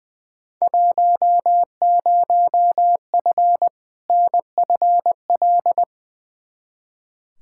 Morse code, heard as a keyed tone, about 20 wpm, 700 Hz. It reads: 10F NFL
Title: Field Day Exchanges Encore - 20wpm